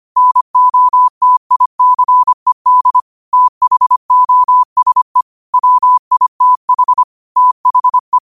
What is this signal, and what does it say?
Morse code, heard as a keyed tone, about 25 wpm, 1000 Hz. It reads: NOTICED THOSE WITH THE
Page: Quotes for Wed, 13 Aug 2025 in Morse Code at 25 words per minute.